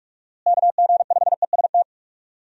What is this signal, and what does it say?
Morse code, heard as a keyed tone, about 46 wpm, 700 Hz. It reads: KG5EST
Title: USA Call Signs - 6 Characters - Repeated 3x - 45wpm